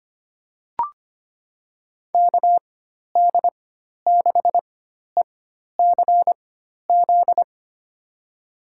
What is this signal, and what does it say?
Morse code, heard as a keyed tone, about 25 wpm, spaced as for 12 wpm, 700 Hz. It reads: KD6ECZ
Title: USA Call Signs - 6 Characters - 4x Character Spacing - 25wpm